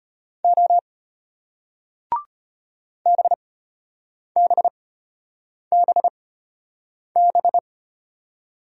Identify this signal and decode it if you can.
Morse code, heard as a keyed tone, about 33 wpm, spaced as for 8 wpm, 700 Hz. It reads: O BBBB